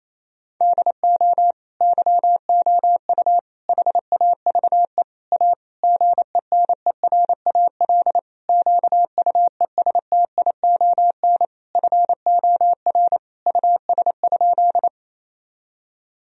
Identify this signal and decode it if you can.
Morse code, heard as a keyed tone, about 28 wpm, 700 Hz. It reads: DO YOU HAVE A GENERAL QUESTION FOR US?
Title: Sentences from Top 1000 Words - Review Part 3 - 28wpm